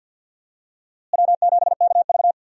Morse code, heard as a keyed tone, about 52 wpm, 700 Hz. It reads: W7KV
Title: USA Call Signs - 4 Characters - Repeated 2x - 50wpm